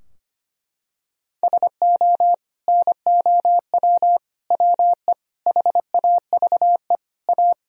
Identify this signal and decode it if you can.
Morse code, heard as a keyed tone, about 25 wpm, 700 Hz. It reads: SO NOW WE HAVE A
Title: Sentences from Top 1000 Words - Review Part 2 - 25wpm